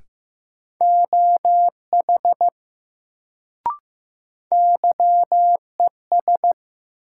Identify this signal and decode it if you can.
Morse code, heard as a keyed tone, about 15 wpm, 700 Hz. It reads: OH YES
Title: Sentences from Top 500 Words - New Word Drill - 15wpm